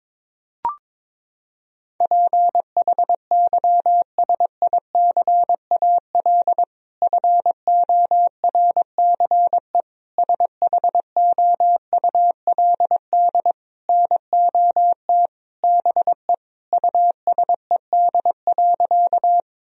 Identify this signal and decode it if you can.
Morse code, heard as a keyed tone, about 22 wpm, 700 Hz. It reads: PHYSICAL FORCE SHOULD NOT BE USED.
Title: Sentences from Top 1000 Words - Review Part 4 - 22wpm